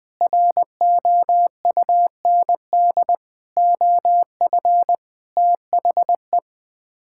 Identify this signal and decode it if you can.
Morse code, heard as a keyed tone, about 20 wpm, 700 Hz. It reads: ROUND OF THE
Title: Sets of 6 Words - 20wpm